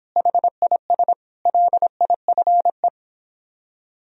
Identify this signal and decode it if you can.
Morse code, heard as a keyed tone, about 26 wpm, 700 Hz. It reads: HIS LIFE